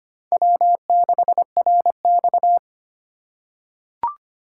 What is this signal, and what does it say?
Morse code, heard as a keyed tone, about 25 wpm, 700 Hz. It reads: W6RX
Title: USA Call Signs - 4 Characters - 4x Character Spacing - 25wpm